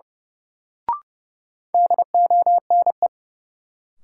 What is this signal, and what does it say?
Morse code, heard as a keyed tone, about 31 wpm, 700 Hz. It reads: DONE
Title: Sets of 5 Words Encore - New Word Drill - 30wpm